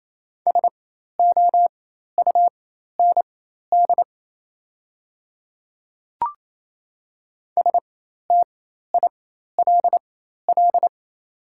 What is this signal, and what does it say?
Morse code, heard as a keyed tone, about 28 wpm, spaced as for 13 wpm, 690 Hz. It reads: SOUND STILL
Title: 4x Character Spacing - Top 500 Words - 28wpm